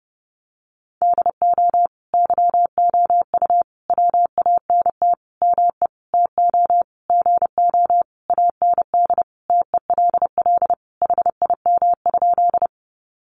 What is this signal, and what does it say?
Morse code, heard as a keyed tone, about 30 wpm, 700 Hz. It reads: DO YOU WANT ME TO GO AND TELL HIM?
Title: Sentences from Top 200 Words 30wpm